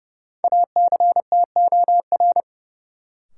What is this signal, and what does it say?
Morse code, heard as a keyed tone, about 30 wpm, 700 Hz. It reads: ACTOR
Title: Sentences from Top 1500 Words - New Word Drill - 30wpm